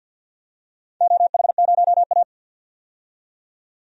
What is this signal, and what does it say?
Morse code, heard as a keyed tone, about 51 wpm, 700 Hz. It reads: OH0A
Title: DXCC-100 Call Signs - 4 Characters - 50wpm